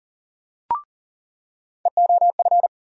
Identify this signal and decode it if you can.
Morse code, heard as a keyed tone, about 40 wpm, 700 Hz. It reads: EOF
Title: USA Call Sign Postfixes - 3 Letters 40wpm